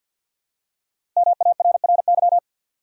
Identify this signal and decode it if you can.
Morse code, heard as a keyed tone, about 53 wpm, 700 Hz. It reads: MARRY
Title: Five Letter Words - 50wpm